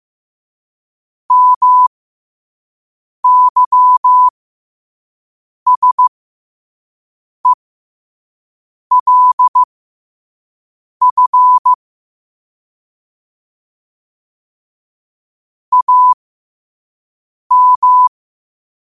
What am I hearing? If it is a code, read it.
Morse code, heard as a keyed tone, about 15 wpm, spaced as for 5 wpm, 1000 Hz. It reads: MYSELF AM
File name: qotdfiles / 5wpm